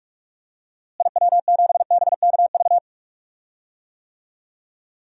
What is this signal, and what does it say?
Morse code, heard as a keyed tone, about 46 wpm, 700 Hz. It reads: IW7BKV
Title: DXCC-100 Call Signs - 6 Characters - Repeated 3x - 45wpm